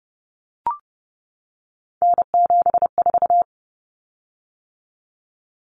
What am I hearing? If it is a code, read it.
Morse code, heard as a keyed tone, about 30 wpm, 700 Hz. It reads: N74
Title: Common Cut Numbers - 3-Digit 30wpm